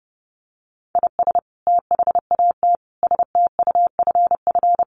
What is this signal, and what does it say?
Morse code, heard as a keyed tone, about 30 wpm, 700 Hz. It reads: IS THAT STUFF
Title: Sentences from Top 1400 Words 30wpm